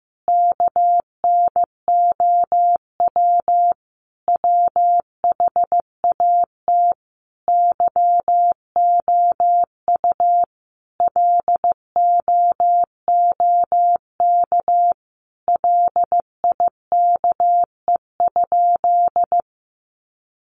Morse code, heard as a keyed tone, about 15 wpm, 700 Hz. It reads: KNOW WHAT YOU LOOK LIKE?
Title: Sentences from Top 100 Words 15wpm